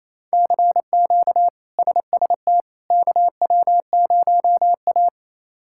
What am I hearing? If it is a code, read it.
Morse code, heard as a keyed tone, about 28 wpm, 700 Hz. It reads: CQ SST KW0A
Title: K1USN SST QSO Practice - 28wpm